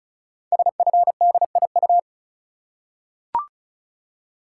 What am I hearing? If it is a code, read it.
Morse code, heard as a keyed tone, about 36 wpm, 700 Hz. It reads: SFDIU
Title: Code Groups - 5 Letters - 35wpm